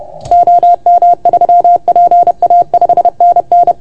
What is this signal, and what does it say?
Morse code, heard as a keyed tone, about 30 wpm, 690 Hz. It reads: OM3PA 5NN
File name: Archiv / CUC381 / OK1FGD CUC 381